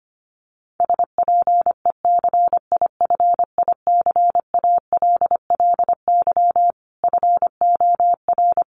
Morse code, heard as a keyed tone, about 25 wpm, 700 Hz. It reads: SPECIFICALLY FOR
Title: Sentences from Top 1100 Words 25wpm